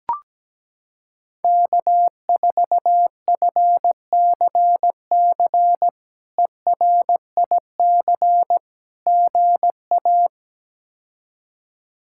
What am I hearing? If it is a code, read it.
Morse code, heard as a keyed tone, about 17 wpm, 700 Hz. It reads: K4FCC ERIC GA
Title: K1USN SST QSO Practice - 17wpm